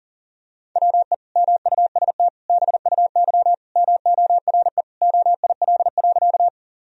Morse code, heard as a keyed tone, about 40 wpm, 700 Hz. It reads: WE MUST BUY MORE OIL.
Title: Sentences from Top 1000 Words - Review Part 4 - 40wpm